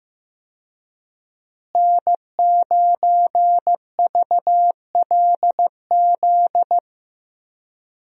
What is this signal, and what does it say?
Morse code, heard as a keyed tone, about 15 wpm, 700 Hz. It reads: N9VLZ
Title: USA Call Signs - 5 Characters - Repeated 2x - 15wpm